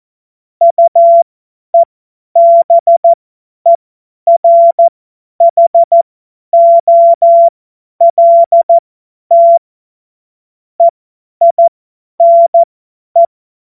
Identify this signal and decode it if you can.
Morse code, heard as a keyed tone, about 14 wpm, spaced as for 10 wpm, 680 Hz. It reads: UEBERHOLT EINE
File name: Morsekurs / Witze / Witz 31.Oktober 2023 50 F70